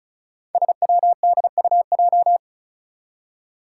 Morse code, heard as a keyed tone, about 34 wpm, 700 Hz. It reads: SWDUJ
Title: Five Challenging Letter-Number Combinations 35wpm